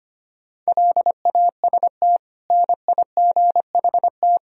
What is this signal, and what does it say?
Morse code, heard as a keyed tone, about 25 wpm, 700 Hz. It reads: LAST NIGHT
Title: Sentences from Top 1000 Words - Review Part 4 - 25wpm